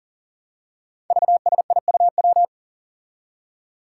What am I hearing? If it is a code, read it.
Morse code, heard as a keyed tone, about 40 wpm, 700 Hz. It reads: VSIUW
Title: Five Challenging Letter-Number Combinations 40wpm